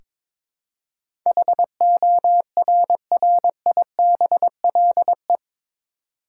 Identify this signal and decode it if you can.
Morse code, heard as a keyed tone, about 22 wpm, 700 Hz. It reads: HORRIBLE
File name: practice / 22wpm / Eight Letter Words - 22wpm